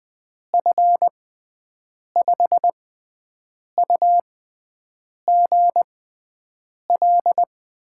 Morse code, heard as a keyed tone, about 20 wpm, spaced as for 7 wpm, 700 Hz. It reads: F5UGL
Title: USA Call Signs - 6 Characters - 6x Character Spacing - 20wpm